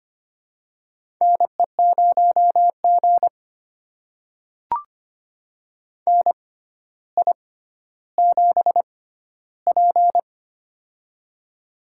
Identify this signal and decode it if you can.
Morse code, heard as a keyed tone, about 25 wpm, spaced as for 13 wpm, 700 Hz. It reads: NE0G NI7P
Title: USA Call Signs - 4 Characters - 6x Character Spacing - 25wpm